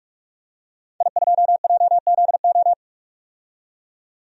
Morse code, heard as a keyed tone, about 46 wpm, 700 Hz. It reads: I2JZO
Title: DXCC-100 Call Signs - 5 Characters - 45wpm